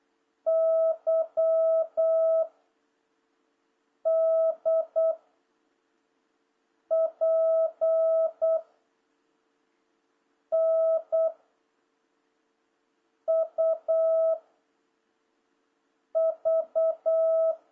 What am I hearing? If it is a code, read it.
Morse code, heard as a keyed tone, about 8 wpm, 660 Hz. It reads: Y D P N U V
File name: jh8cbh / 2026cw